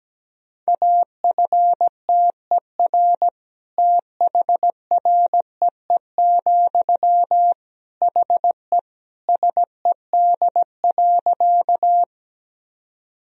Morse code, heard as a keyed tone, about 17 wpm, 700 Hz. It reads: AFTER THREE, HE SED.
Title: Sentences from Top 200 Words and Abbreviations 17wpm